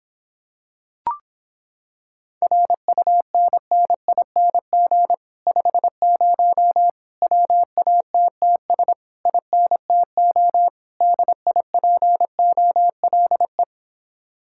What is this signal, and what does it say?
Morse code, heard as a keyed tone, about 26 wpm, 700 Hz. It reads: RUNNING 50 WATTS INTO DIPOLE